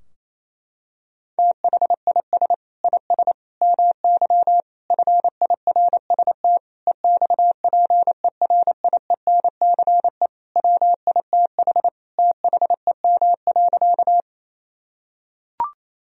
Morse code, heard as a keyed tone, about 27 wpm, 700 Hz. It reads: THIS IS MY FIRST EXPERIENCE WITH THEM.
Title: Sentences from Top 1000 Words - Review Part 3 - 28wpm